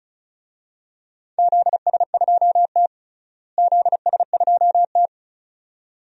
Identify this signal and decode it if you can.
Morse code, heard as a keyed tone, about 34 wpm, 700 Hz. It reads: ZS2T ZS2T
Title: DXCC-100 Call Signs - 4 Characters - Repeated 2x - 35wpm